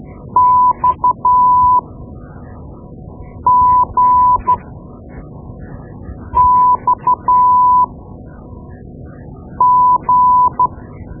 Morse code, heard as a keyed tone, about 12 wpm, spaced as for 4 wpm, 1000 Hz. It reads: XGXG
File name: Morse Lessons / Morse Lessons / Tape-2B